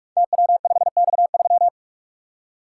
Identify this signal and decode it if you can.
Morse code, heard as a keyed tone, about 44 wpm, 700 Hz. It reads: TW5X3
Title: Code Groups - 5 Letter:Numbers - 45wpm